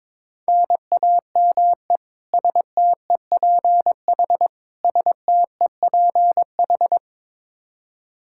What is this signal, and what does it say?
Morse code, heard as a keyed tone, about 22 wpm, 700 Hz. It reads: NAME STEPH STEPH
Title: Practice QSO - Exchange 1 - Beginning - 22wpm